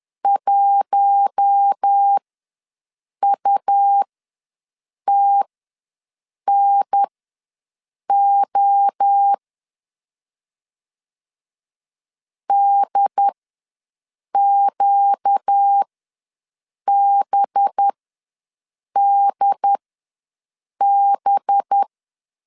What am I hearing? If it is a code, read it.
Morse code, heard as a keyed tone, about 10 wpm, spaced as for 6 wpm, 800 Hz. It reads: 1UTNO DQBDB
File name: LEZIONI / 5 LBVC / 5 LB ws 10wpm 3616